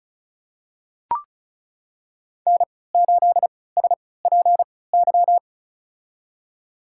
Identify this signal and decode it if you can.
Morse code, heard as a keyed tone, about 34 wpm, spaced as for 20 wpm, 700 Hz. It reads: N8SPY
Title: USA Call Signs - 5 Characters - 3x Character Spacing - 35wpm